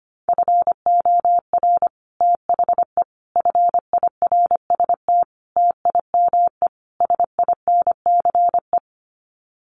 Morse code, heard as a keyed tone, about 25 wpm, 700 Hz. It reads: FOR THE FIRST TIME SINCE
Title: Sets of 5 Words 25wpm